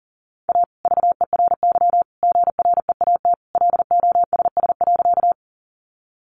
Morse code, heard as a keyed tone, about 40 wpm, 700 Hz. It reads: A VERY GREAT LOSS.
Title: Sentences from Top 1400 Words 40wpm